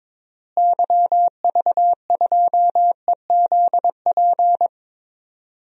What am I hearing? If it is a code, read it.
Morse code, heard as a keyed tone, about 22 wpm, 700 Hz. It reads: YV2EZP
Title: DXCC-100 Call Signs - 6 Characters - Repeated 3x - 22wpm